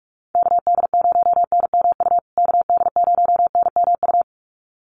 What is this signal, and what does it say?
Morse code, heard as a keyed tone, about 44 wpm, 700 Hz. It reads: KD0NMU KD0NMU
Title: USA Call Signs Repeated 2x 45wpm